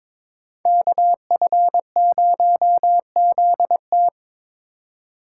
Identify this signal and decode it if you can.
Morse code, heard as a keyed tone, about 26 wpm, 700 Hz. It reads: KF0ZT
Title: Speed Racing - USA 2x2 Call Signs 22wpm